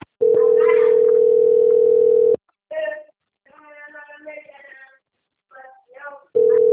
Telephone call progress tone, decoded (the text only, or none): ringback tone